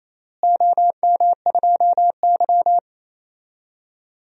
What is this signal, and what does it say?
Morse code, heard as a keyed tone, about 28 wpm, 700 Hz. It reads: OM2Y
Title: DXCC-100 Call Signs - 4 Characters - Repeated 2x - 28wpm